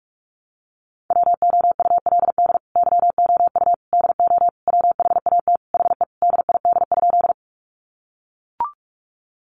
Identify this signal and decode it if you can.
Morse code, heard as a keyed tone, about 44 wpm, 700 Hz. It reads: WOULD YOU DO WHAT HE DID?
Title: Sentences from Top 100 Words 45wpm